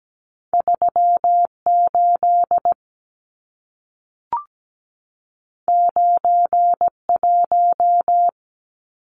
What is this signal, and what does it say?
Morse code, heard as a keyed tone, about 17 wpm, 700 Hz. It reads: 38 91
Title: Numbers - 2-Digit 17wpm